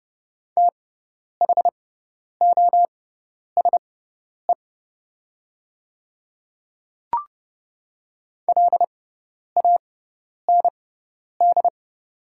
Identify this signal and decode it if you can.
Morse code, heard as a keyed tone, about 30 wpm, spaced as for 10 wpm, 700 Hz. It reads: THOSE LAND
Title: 6x Character Spacing - Top 300 Words - 30wpm